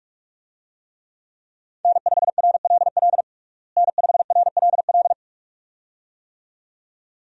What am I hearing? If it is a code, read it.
Morse code, heard as a keyed tone, about 41 wpm, 700 Hz. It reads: N5RLL N5RLL
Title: USA Call Signs - 5 Characters - Repeated 2x - 45wpm